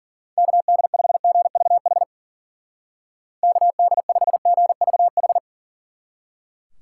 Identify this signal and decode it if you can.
Morse code, heard as a keyed tone, about 49 wpm, 700 Hz. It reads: KD5GVH KD5GVH
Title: Speed Racing - USA 2x3 Call Signs 40wpm